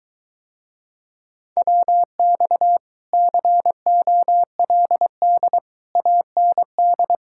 Morse code, heard as a keyed tone, about 23 wpm, 700 Hz. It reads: WX COLD AND